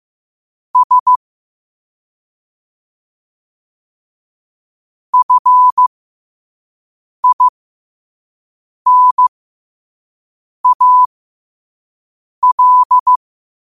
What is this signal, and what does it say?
Morse code, heard as a keyed tone, about 15 wpm, spaced as for 5 wpm, 1000 Hz. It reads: S FINAL
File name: qotdfiles / 5wpm